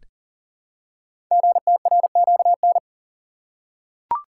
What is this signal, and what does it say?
Morse code, heard as a keyed tone, about 40 wpm, 700 Hz. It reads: GTRQN
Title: Code Groups - 5 Letters - 40wpm